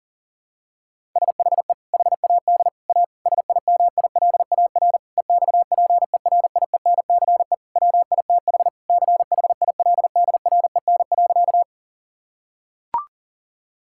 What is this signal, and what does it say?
Morse code, heard as a keyed tone, about 40 wpm, 700 Hz. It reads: SHE HAD A SIMILAR EXPERIENCE WITH CHILDREN.
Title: Sentences from Top 1000 Words - Review Part 5 - 40wpm